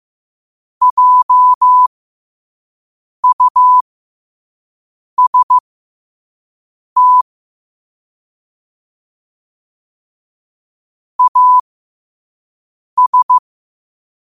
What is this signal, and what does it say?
Morse code, heard as a keyed tone, about 15 wpm, spaced as for 5 wpm, 1000 Hz. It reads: JUST AS
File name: qotdfiles / 5wpm